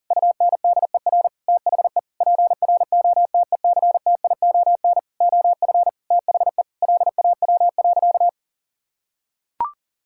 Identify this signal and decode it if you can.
Morse code, heard as a keyed tone, about 37 wpm, 700 Hz. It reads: UNDER THE PROTECTION OF THE LAW.
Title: Sentences from Top 1600 Words 40wpm